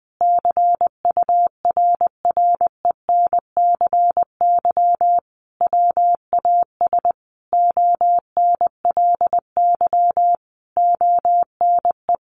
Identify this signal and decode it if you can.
Morse code, heard as a keyed tone, about 19 wpm, 700 Hz. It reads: CURRENCY WAS ONLY ONE
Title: Sentences from Top 1500 Words 20wpm